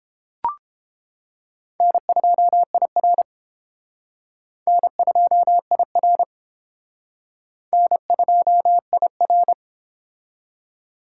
Morse code, heard as a keyed tone, about 30 wpm, 700 Hz. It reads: N2IR N2IR N2IR